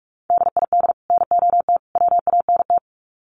Morse code, heard as a keyed tone, about 44 wpm, 700 Hz. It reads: DID NOT WANT
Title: Sets of 3 Words 45wpm